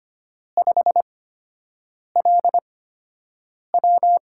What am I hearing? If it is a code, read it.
Morse code, heard as a keyed tone, about 25 wpm, spaced as for 7 wpm, 700 Hz. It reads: 5LW
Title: USA Call Signs - 8x Character Spacing - 25wpm